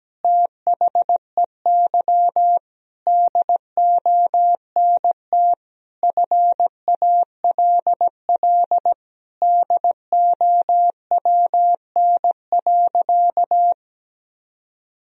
Morse code, heard as a keyed tone, about 17 wpm, 700 Hz. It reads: THEY DONT FALL DOWN.
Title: Sentences from Top 1000 Words - Review Part 2 - 17wpm